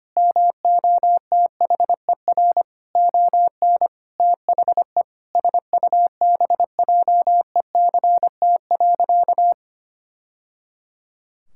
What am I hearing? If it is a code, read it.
Morse code, heard as a keyed tone, about 25 wpm, 700 Hz. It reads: MOTHER ON THE SUBJECT.
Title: Sentences from Top 1000 Words - Review Part 6 - 25wpm